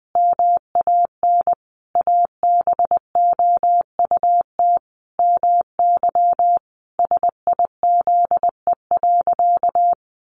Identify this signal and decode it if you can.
Morse code, heard as a keyed tone, about 20 wpm, 700 Hz. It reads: MAN ABOUT MY SIZE.
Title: Sentences from Top 400 Words 20wpm